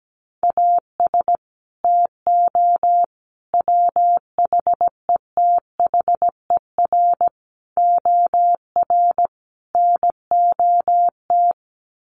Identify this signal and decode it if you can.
Morse code, heard as a keyed tone, about 17 wpm, 700 Hz. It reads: AS TO WHETHER OR NOT
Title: Sets of 5 Words Encore 17wpm